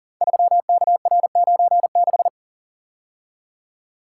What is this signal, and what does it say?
Morse code, heard as a keyed tone, about 40 wpm, 700 Hz. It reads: 3KR96
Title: Code Groups - 5 Letter:Numbers - 40wpm